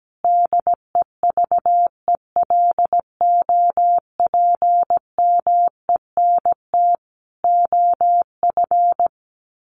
Morse code, heard as a keyed tone, about 17 wpm, 700 Hz. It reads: DEVELOPMENT OF
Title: Sets of 3 Words 17wpm